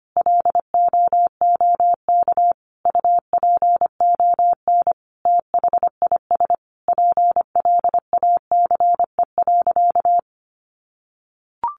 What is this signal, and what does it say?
Morse code, heard as a keyed tone, about 25 wpm, 700 Hz. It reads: LOOK UPON THIS PLACE.
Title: Sentences from Top 1100 Words 25wpm